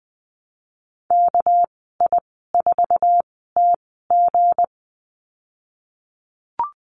Morse code, heard as a keyed tone, about 20 wpm, spaced as for 15 wpm, 700 Hz. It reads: KI4TG
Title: 245 Lesson 39g - Letter z - Call Signs 20wpm